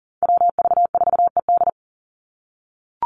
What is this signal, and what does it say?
Morse code, heard as a keyed tone, about 40 wpm, 700 Hz. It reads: WV4ED
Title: USA Call Signs 40wpm